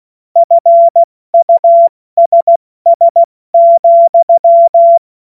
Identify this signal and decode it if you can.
Morse code, heard as a keyed tone, about 16 wpm, 680 Hz. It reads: FUSS,
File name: Witz-08-Juli-2025-70-F80